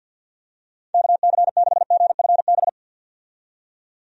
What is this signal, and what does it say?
Morse code, heard as a keyed tone, about 47 wpm, 700 Hz. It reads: KC6GFB
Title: USA Call Signs - 6 Characters - 50wpm